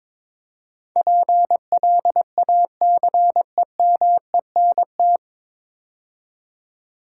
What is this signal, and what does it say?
Morse code, heard as a keyed tone, about 22 wpm, 700 Hz. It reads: PLACEMENT
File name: practice / 22wpm / 9 Letter Words - First 3 Letters Given - 22wpm